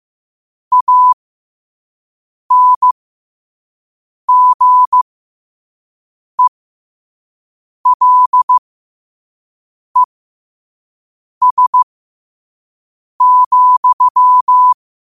Morse code, heard as a keyed tone, about 15 wpm, spaced as for 5 wpm, 1000 Hz. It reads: ANGELES,
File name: qotdfiles / 5wpm